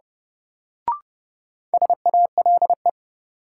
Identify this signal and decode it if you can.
Morse code, heard as a keyed tone, about 32 wpm, 700 Hz. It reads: SALE
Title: Sentences from Top 1400 Words - New Word Drill Alt - 30wpm